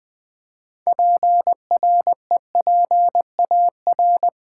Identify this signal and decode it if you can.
Morse code, heard as a keyed tone, about 20 wpm, 700 Hz. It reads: PREPAR
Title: Build A Word - 3 Letters At A Time - 8-Letter Words - 20wpm